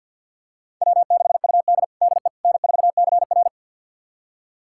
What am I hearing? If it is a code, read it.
Morse code, heard as a keyed tone, about 52 wpm, 700 Hz. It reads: W6UD DE N4CR
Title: USA Callsign DE Callsign - 4 Characters - 50wpm